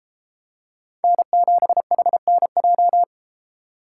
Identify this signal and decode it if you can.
Morse code, heard as a keyed tone, about 32 wpm, 700 Hz. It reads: N7HNJ